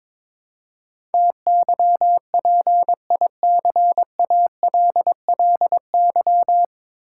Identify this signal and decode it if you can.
Morse code, heard as a keyed tone, about 23 wpm, 700 Hz. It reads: TYPICALLY
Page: Morse Code Ninja practice 22wpm